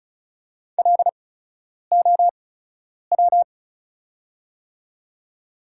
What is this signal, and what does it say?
Morse code, heard as a keyed tone, about 36 wpm, spaced as for 10 wpm, 700 Hz. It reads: LOW
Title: 8x Character Spacing - Top 200 Words - 35wpm